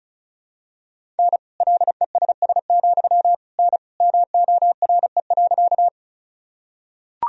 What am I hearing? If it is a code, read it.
Morse code, heard as a keyed tone, about 34 wpm, 700 Hz. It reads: N LESS, N MORE.
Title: Sentences from Top 400 Words and Abbreviations 35wpm